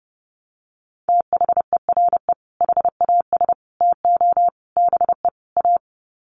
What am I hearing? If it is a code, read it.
Morse code, heard as a keyed tone, about 30 wpm, 700 Hz. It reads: THERE HAS TO BE A
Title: Sets of 5 Words Encore 30wpm